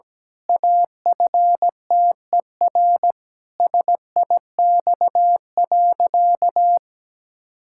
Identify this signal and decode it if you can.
Morse code, heard as a keyed tone, about 17 wpm, 700 Hz. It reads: AFTER SIX.
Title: Pangram Challenge v2 - 17wpm